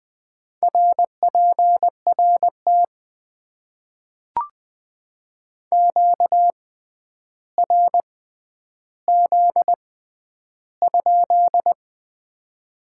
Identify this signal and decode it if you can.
Morse code, heard as a keyed tone, about 20 wpm, spaced as for 10 wpm, 700 Hz. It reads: RPRT QRZ?
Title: Top 90 QSO Elements - 6x Character Spacing - 20wpm